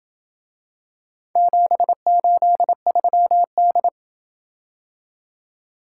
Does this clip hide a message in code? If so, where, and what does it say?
Yes: Morse code from 1.4 to 3.9 s, heard as a keyed tone, about 27 wpm, 700 Hz. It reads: 783D